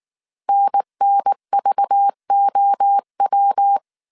Morse code, heard as a keyed tone, about 19 wpm, 800 Hz. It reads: NNVOW
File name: CW / DET / 1 6 1500C S A 18wpm 2255